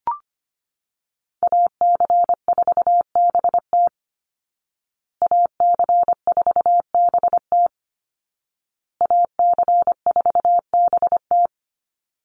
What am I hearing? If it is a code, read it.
Morse code, heard as a keyed tone, about 25 wpm, 700 Hz. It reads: AC4BT AC4BT AC4BT
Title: Call Signs Repeated 2x - 25wpm